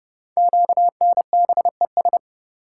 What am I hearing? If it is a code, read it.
Morse code, heard as a keyed tone, about 28 wpm, 700 Hz. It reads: QNBES
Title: Code Groups - 5 Letters - 30wpm